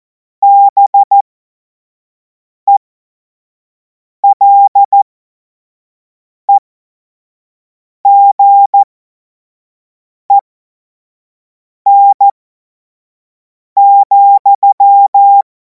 Morse code, heard as a keyed tone, about 14 wpm, spaced as for 5 wpm, 800 Hz. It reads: BELEGEN,
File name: Morsekurs / Witze / Witz 09. Juni 2026 25 F70